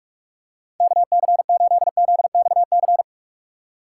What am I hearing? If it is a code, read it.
Morse code, heard as a keyed tone, about 46 wpm, 690 Hz. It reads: KC8ZXC
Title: USA Call Signs - 4x Character Spacing - 45wpm